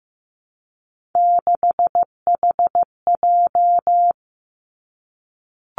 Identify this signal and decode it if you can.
Morse code, heard as a keyed tone, about 15 wpm, 700 Hz. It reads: 6HJ
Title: Three Challenging Letter-Number Combinations 15wpm